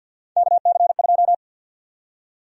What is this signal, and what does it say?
Morse code, heard as a keyed tone, about 51 wpm, 700 Hz. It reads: KC2
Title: USA Call Sign Prefixes - 50wpm